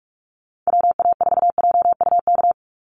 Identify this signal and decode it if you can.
Morse code, heard as a keyed tone, about 44 wpm, 700 Hz. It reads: WA4JUK
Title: Call Signs Repeated 3x - 45wpm